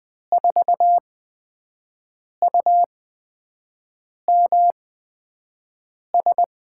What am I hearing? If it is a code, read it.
Morse code, heard as a keyed tone, about 20 wpm, spaced as for 5 wpm, 700 Hz. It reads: 4UMS
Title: USA Call Signs - 8x Character Spacing - 20wpm